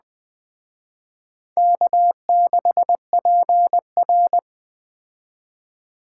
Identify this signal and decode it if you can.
Morse code, heard as a keyed tone, about 20 wpm, 700 Hz. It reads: K6PR
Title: Speed Racing - USA 1x2 Call Signs 17wpm